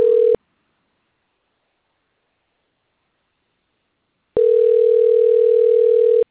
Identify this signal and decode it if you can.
Ringback tone (North America), 440 + 480 Hz.
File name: rg-1600-unknown- - - .91429